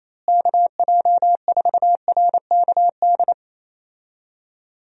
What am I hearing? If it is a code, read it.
Morse code, heard as a keyed tone, about 27 wpm, 700 Hz. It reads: KJ4RKD
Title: USA Call Signs - 6 Characters - Repeated 3x - 28wpm